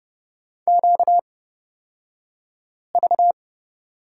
Morse code, heard as a keyed tone, about 30 wpm, spaced as for 5 wpm, 700 Hz. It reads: QV
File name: Lesson 38d - Letter q - Two Characters 30wpm